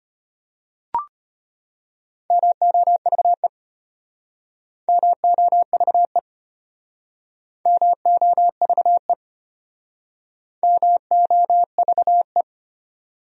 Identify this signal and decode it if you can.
Morse code, heard as a keyed tone, about 33 wpm, 700 Hz. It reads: MOVE MOVE MOVE MOVE